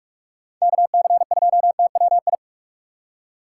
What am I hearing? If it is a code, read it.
Morse code, heard as a keyed tone, about 59 wpm, spaced as for 52 wpm, 700 Hz. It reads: KC2TWI
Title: Speed Racing - USA Call Signs 45wpm